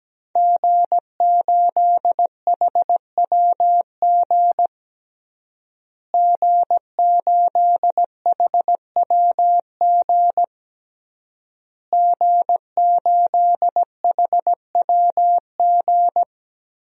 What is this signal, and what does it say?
Morse code, heard as a keyed tone, about 17 wpm, 700 Hz. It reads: G8HWG G8HWG G8HWG
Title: DXCC-100 Call Signs - 5 Characters - Repeated 3x - 17wpm